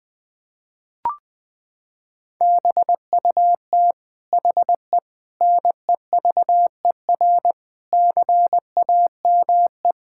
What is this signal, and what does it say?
Morse code, heard as a keyed tone, about 20 wpm, 700 Hz. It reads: BUT HE NEVER CAME
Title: Sentences from Top 1600 Words 20wpm